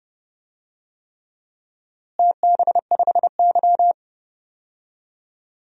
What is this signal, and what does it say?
Morse code, heard as a keyed tone, about 30 wpm, 700 Hz. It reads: TB5Y
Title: DXCC-100 Call Signs - 4 Characters - Repeated 3x - 30wpm